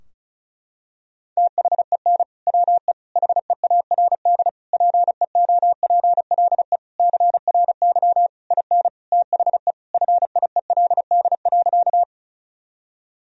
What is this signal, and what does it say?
Morse code, heard as a keyed tone, about 34 wpm, 700 Hz. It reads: THEN WE HEARD PEOPLE CRY IN THE FIELD.
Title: Sentences from Top 1000 Words - Review Part 2 - 35wpm